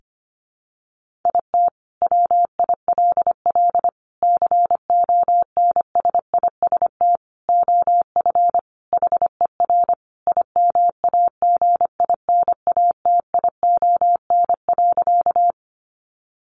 Morse code, heard as a keyed tone, about 25 wpm, 700 Hz. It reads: IT WILL CONSIST OF HER IMAGINATION.
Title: Sentences from Top 1400 Words 25wpm